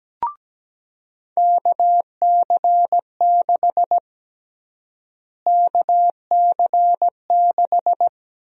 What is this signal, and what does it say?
Morse code, heard as a keyed tone, about 17 wpm, 700 Hz. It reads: KC6 KC6
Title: USA Call Sign Prefixes - Repeated 2x - 17wpm